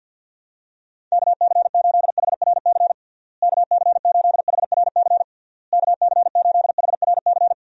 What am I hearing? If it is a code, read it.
Morse code, heard as a keyed tone, about 52 wpm, 700 Hz. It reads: KC8HRC KC8HRC KC8HRC
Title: USA Call Signs - 6 Characters - Repeated 3x - 50wpm